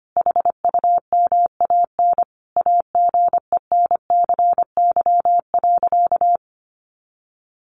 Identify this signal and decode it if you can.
Morse code, heard as a keyed tone, about 25 wpm, 700 Hz. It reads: HUMAN AGENCY.
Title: Sentences from Top 1300 Words 25wpm